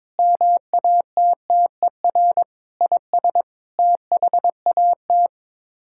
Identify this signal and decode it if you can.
Morse code, heard as a keyed tone, about 21 wpm, 700 Hz. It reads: MATTER IS THAT
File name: practice / 22wpm / Sets of 8 Words - 22wpm